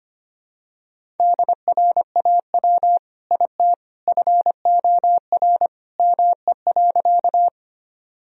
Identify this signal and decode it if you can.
Morse code, heard as a keyed tone, about 25 wpm, 700 Hz. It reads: DRAW IT FOR ME.
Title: Sentences from Top 1000 Words - Review Part 1 - 25wpm